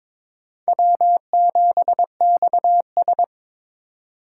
Morse code, heard as a keyed tone, about 22 wpm, 700 Hz. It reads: W7XS
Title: USA Call Signs - 4 Characters - 2x Character Spacing - 22wpm